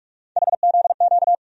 Morse code, heard as a keyed tone, about 46 wpm, 700 Hz. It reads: HZQ
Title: USA Call Sign Postfixes - 3 Letters 45wpm